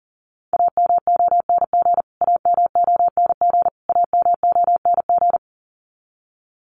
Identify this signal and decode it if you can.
Morse code, heard as a keyed tone, about 40 wpm, 700 Hz. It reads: AMONG AMONG AMONG
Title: Top 500 Words Repeated 40wpm